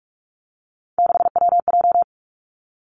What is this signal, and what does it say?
Morse code, heard as a keyed tone, about 46 wpm, 700 Hz. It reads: 6WJ
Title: Three Challenging Letter-Number Combinations 45wpm